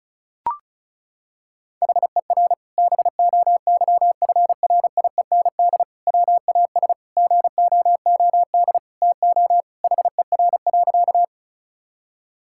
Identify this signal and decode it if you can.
Morse code, heard as a keyed tone, about 36 wpm, 700 Hz. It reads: HER BOYFRIEND WAS GOOD TO HER.
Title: Sentences from Top 1000 Words - Review Part 5 - 35wpm